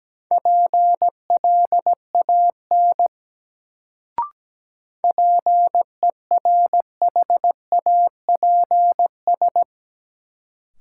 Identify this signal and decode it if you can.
Morse code, heard as a keyed tone, about 17 wpm, 700 Hz. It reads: PLAN PERHAPS
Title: Sentences from Top 500 Words - New Word Drill - 17wpm